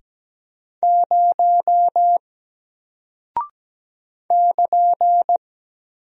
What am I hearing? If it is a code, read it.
Morse code, heard as a keyed tone, about 17 wpm, 700 Hz. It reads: 0 <KN>
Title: Single Character (Letter:Number:Punctuation:Prosign) Alt 17wpm